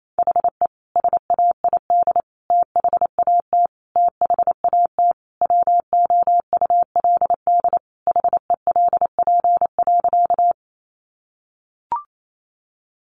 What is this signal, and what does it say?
Morse code, heard as a keyed tone, about 28 wpm, 700 Hz. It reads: HE SAID THAT THAT WOULD HELP.
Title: Sentences from Top 200 Words 28wpm